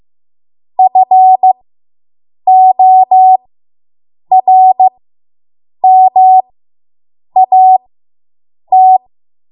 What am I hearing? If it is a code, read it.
Morse code, heard as a keyed tone, about 15 wpm, spaced as for 7 wpm, 750 Hz. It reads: FORMAT